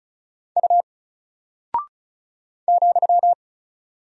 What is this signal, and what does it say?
Morse code, heard as a keyed tone, about 36 wpm, 700 Hz. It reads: U ,
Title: Single Character (Letter:Number:Punctuation:Prosign) 35wpm